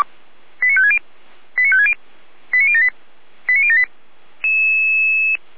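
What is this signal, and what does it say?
Five-tone selective call (ZVEI), heard as quick runs of five tones, tones of about 70 ms. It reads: 89560 89560 89078 89078
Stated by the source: ZVEI five-tone selective call 89170, received on SDR.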